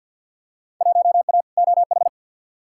Morse code, heard as a keyed tone, about 53 wpm, 700 Hz. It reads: 1A OH
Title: Field Day Exchanges - 50wpm